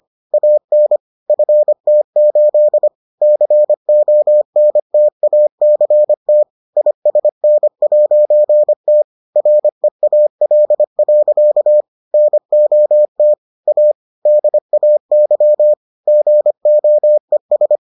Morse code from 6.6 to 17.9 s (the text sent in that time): ISN'T REAL. NOT A DAY GOES